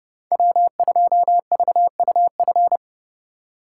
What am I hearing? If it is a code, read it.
Morse code, heard as a keyed tone, about 30 wpm, 700 Hz. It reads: W2VUF
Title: K1USN SST QSO Practice - 30wpm